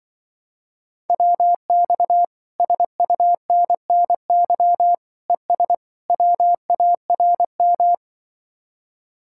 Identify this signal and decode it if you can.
Morse code, heard as a keyed tone, about 24 wpm, 700 Hz. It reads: WX SUNNY ES WARM
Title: Speed-Racing Common QSO Phrases - 20wpm